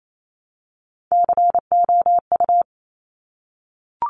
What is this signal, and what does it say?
Morse code, heard as a keyed tone, about 28 wpm, 700 Hz. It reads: COU
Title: Top Three Letter Combinations in Words 28wpm